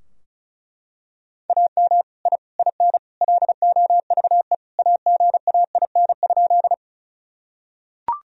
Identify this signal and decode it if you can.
Morse code, heard as a keyed tone, about 36 wpm, 700 Hz. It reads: AM I IN LOVE AGAIN?
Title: Sentences from Top 1000 Words - Review Part 2 - 35wpm